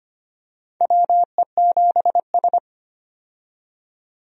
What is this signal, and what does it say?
Morse code, heard as a keyed tone, about 25 wpm, 700 Hz. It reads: WE7S
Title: USA Call Signs - 4 Characters - 25wpm